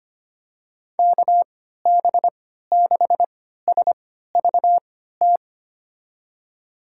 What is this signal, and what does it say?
Morse code, heard as a keyed tone, about 25 wpm, spaced as for 14 wpm, 700 Hz. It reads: KB6SVT
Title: USA Call Signs - 6 Characters - 3x Character Spacing - 25wpm